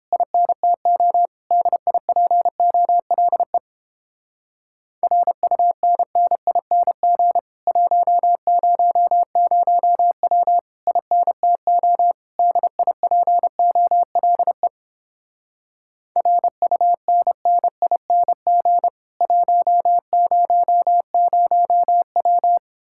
Morse code, heard as a keyed tone, about 31 wpm, 700 Hz. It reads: INTO DIPOLE RUNNING 100W INTO DIPOLE RUNNING 100W